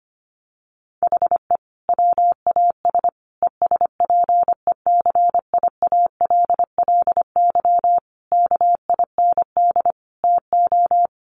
Morse code, heard as a keyed tone, about 25 wpm, 700 Hz. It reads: HE WAS ESPECIALLY KIND TO
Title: Sentences from Top 700 Words 25wpm